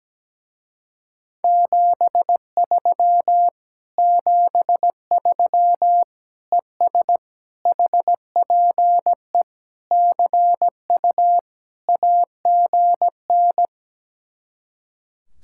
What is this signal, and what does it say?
Morse code, heard as a keyed tone, about 17 wpm, 700 Hz. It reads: 73 73 ES HPE CU AGN
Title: Speed-Racing Common QSO Phrases - 17wpm